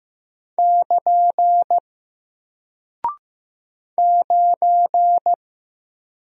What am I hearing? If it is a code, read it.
Morse code, heard as a keyed tone, about 15 wpm, 700 Hz. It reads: <KN> 9
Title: Single Character (Letter:Number:Punctuation:Prosign) Alt 15wpm